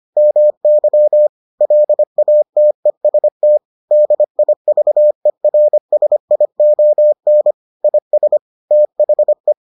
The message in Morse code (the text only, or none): MY LATEST DIVERSION IS THE